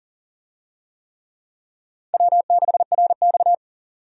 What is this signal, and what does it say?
Morse code, heard as a keyed tone, about 40 wpm, 700 Hz. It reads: W6RX
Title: USA Call Signs - 4 Characters - 40wpm